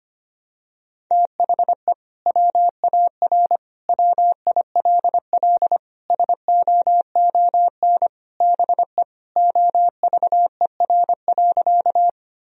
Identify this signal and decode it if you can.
Morse code, heard as a keyed tone, about 25 wpm, 700 Hz. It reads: THE WAR WILL SOON BE OVER.
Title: Sentences from Top 1000 Words - Review Part 2 - 25wpm